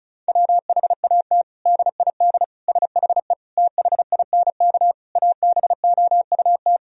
Morse code, heard as a keyed tone, about 36 wpm, 700 Hz. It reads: WHAT DID SHE THINK ABOUT
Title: Sentences from Top 1000 Words - Review Part 4 - 35wpm